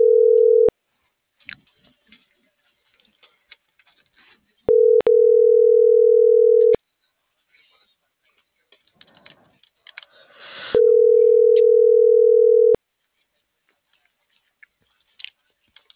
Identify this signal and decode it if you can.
Ringback tone (North America), 440 + 480 Hz.